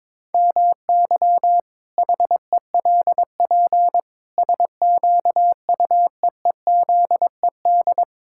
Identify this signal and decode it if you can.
Morse code, heard as a keyed tone, about 22 wpm, 700 Hz. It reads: MY HELP SQUEEZED
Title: Pangram Challenge v2 - 22wpm